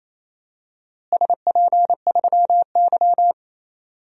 Morse code, heard as a keyed tone, about 28 wpm, 700 Hz. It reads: SP3Y
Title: DXCC-100 Call Signs - 4 Characters - Repeated 3x - 28wpm